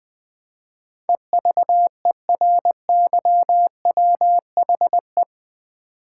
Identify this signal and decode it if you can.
Morse code, heard as a keyed tone, about 20 wpm, 700 Hz. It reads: EVERYWHE
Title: Build A Word - 4 Letters At A Time - 10-Letter Words - 20wpm